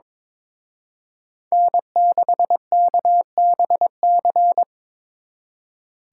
Morse code, heard as a keyed tone, about 26 wpm, 700 Hz. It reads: N6KBC
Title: Speed Racing - USA 1x3 Call Signs 22wpm